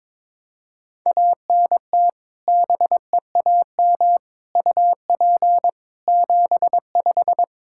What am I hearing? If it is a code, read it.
Morse code, heard as a keyed tone, about 22 wpm, 700 Hz. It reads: ANT BEAM UP 75
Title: Practice QSO - Exchange 2 - Info - 22wpm